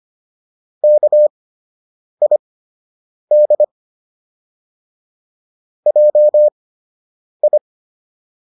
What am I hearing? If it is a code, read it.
Morse code, heard as a keyed tone, about 25 wpm, spaced as for 8 wpm, 600 Hz. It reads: KID JI